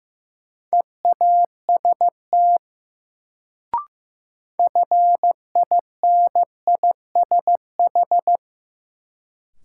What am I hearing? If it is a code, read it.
Morse code, heard as a keyed tone, about 15 wpm, 700 Hz. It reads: EAST FINISH
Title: Sentences from Top 1200 Words - New Word Drill Alt - 15wpm